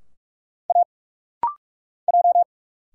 Morse code, heard as a keyed tone, about 45 wpm, 700 Hz. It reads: A J
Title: Single Letters - Rapid-Fire 45wpm